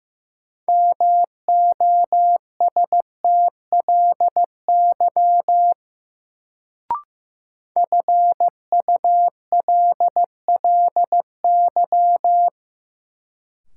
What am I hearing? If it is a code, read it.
Morse code, heard as a keyed tone, about 15 wpm, 700 Hz. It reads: MOSTLY FULLY
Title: Sentences from Top 1300 Words - New Word Drill Alt - 15wpm